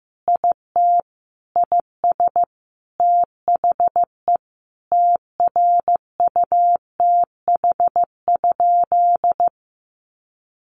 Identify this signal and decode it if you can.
Morse code, heard as a keyed tone, about 15 wpm, 700 Hz. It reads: IT IS THE TRUTH?
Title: Sentences from Top 1000 Words 15wpm